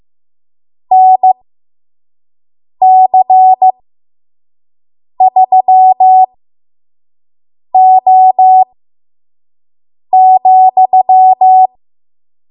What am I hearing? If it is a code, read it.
Morse code, heard as a keyed tone, about 15 wpm, spaced as for 5 wpm, 750 Hz. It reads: NC3O,
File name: Morse / 05WPM